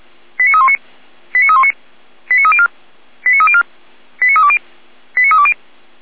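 Five-tone selective call (ZVEI), heard as quick runs of five tones, tones of about 70 ms. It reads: 89319 89319 89304 89304 89320 89320
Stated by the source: ZVEI five-tone selective call received by SDR.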